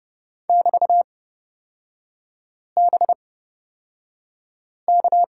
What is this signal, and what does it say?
Morse code, heard as a keyed tone, about 30 wpm, spaced as for 5 wpm, 700 Hz. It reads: <BT>BK
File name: Lesson 36e - BT - Three Characters 30wpm